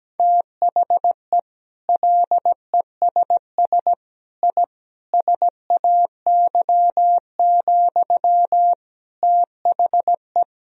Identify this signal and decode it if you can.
Morse code, heard as a keyed tone, about 17 wpm, 700 Hz. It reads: THE LESS I SAY, THE
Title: Sentences from Top 1000 Words - Review Part 2 - 17wpm